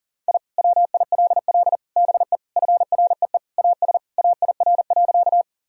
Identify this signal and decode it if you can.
Morse code, heard as a keyed tone, about 40 wpm, 700 Hz. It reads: I WILL BE FREE AS AIR.
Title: Sentences from Top 1000 Words - Review Part 1 - 40wpm